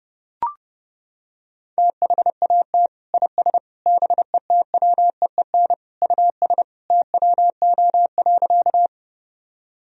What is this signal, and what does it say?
Morse code, heard as a keyed tone, about 32 wpm, 700 Hz. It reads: THAT IS BETWEEN US TWO.
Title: Sentences from Top 1000 Words - Review Part 1 - 30wpm